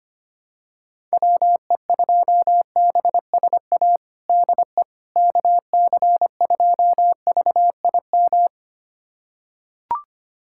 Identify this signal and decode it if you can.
Morse code, heard as a keyed tone, about 25 wpm, 700 Hz. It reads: WE2BSA DE KC2VIM
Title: USA Callsign DE Callsign - 6 Characters - 25wpm